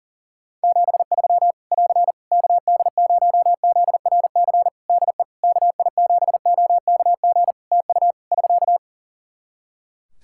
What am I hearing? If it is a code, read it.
Morse code, heard as a keyed tone, about 39 wpm, 700 Hz. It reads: 73 <AR> KD0ZRC DE KI7OKG TU <SK>
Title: K1USN SST QSO Practice 40wpm